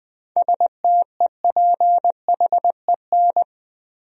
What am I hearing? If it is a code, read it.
Morse code, heard as a keyed tone, about 20 wpm, 700 Hz. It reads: STEPHEN
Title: USA Common Names - 20wpm